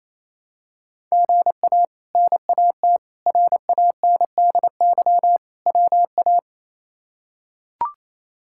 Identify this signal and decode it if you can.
Morse code, heard as a keyed tone, about 28 wpm, 700 Hz. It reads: GA NAT RANDY WA
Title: K1USN SST QSO Practice - 28wpm